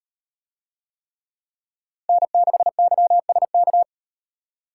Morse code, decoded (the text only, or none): N6YSK